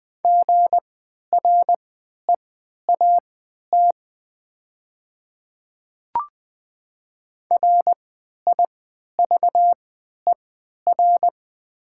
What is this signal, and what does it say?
Morse code, heard as a keyed tone, about 20 wpm, spaced as for 11 wpm, 700 Hz. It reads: GREAT RIVER
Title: 3x Character Spacing - Top 300 Words - 20wpm